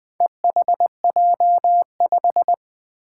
Morse code, heard as a keyed tone, about 20 wpm, 700 Hz. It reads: EHJ5
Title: Four Challenging Letter-Number Combinations 20wpm